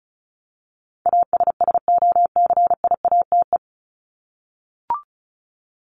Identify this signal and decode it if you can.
Morse code, heard as a keyed tone, about 34 wpm, 700 Hz. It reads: ASSOCIATE
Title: ED Suffix - Top 100 Words with Base Words 35wpm